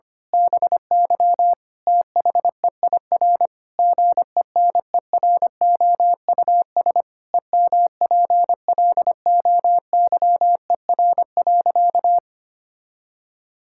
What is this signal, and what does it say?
Morse code, heard as a keyed tone, about 25 wpm, 700 Hz. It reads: BY THEIR GENEROUS EMPLOYER.
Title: Pangram Challenge v2 - 25wpm